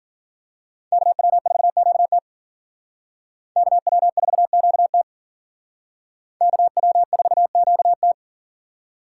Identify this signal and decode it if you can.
Morse code, heard as a keyed tone, about 47 wpm, 700 Hz. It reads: KW4QT KW4QT KW4QT
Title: Speed Racing - USA Call Signs 40wpm